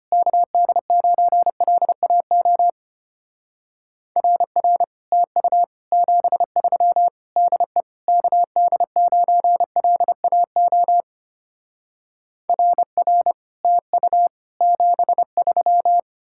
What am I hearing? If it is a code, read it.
Morse code, heard as a keyed tone, about 32 wpm, 700 Hz. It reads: KD9LAO RR TU 73 DE KD9LAO RR TU 73